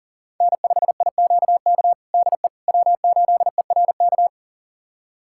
Morse code, heard as a keyed tone, about 40 wpm, 700 Hz. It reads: N5IQK DE W8ERK
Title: USA Callsign DE Callsign - 5 Characters - 40wpm